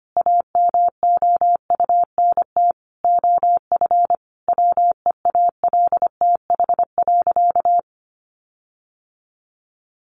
Morse code, heard as a keyed tone, about 25 wpm, 700 Hz. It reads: AMOUNT OF WEALTH.
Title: Sentences from Top 1300 Words 25wpm